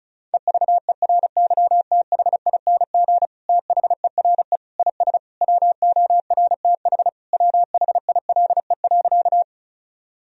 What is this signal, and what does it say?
Morse code, heard as a keyed tone, about 36 wpm, 700 Hz. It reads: EVERYTHING THERE IS WORTH WHILE.
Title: Sentences from Top 1000 Words - Review Part 1 - 35wpm